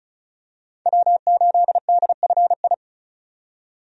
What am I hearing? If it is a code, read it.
Morse code, heard as a keyed tone, about 36 wpm, 700 Hz. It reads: W8DFI
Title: USA Call Signs - 5 Characters - 35wpm